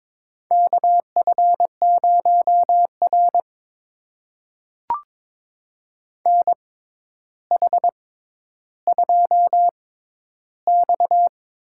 Morse code, heard as a keyed tone, about 23 wpm, spaced as for 11 wpm, 700 Hz. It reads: KF0R NH2X
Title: USA Call Signs - 4 Characters - 6x Character Spacing - 22wpm